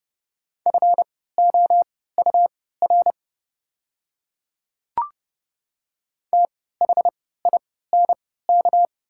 Morse code, heard as a keyed tone, about 32 wpm, spaced as for 17 wpm, 690 Hz. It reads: FOUR THINK
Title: 3x Character Spacing - Top 300 Words - 30wpm